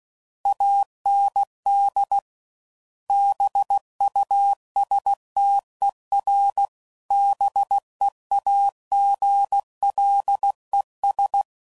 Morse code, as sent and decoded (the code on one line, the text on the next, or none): .- -. -.. / -... ..- ... - . .-. / -... .- --. .-.. ...
AND BUSTER BEAGLES